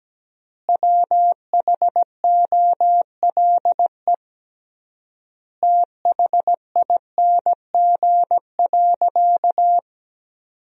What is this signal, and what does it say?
Morse code, heard as a keyed tone, about 17 wpm, 700 Hz. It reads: WHOLE THING.
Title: 3x Word-Spacing - Sentences from Top 400 Words - 17wpm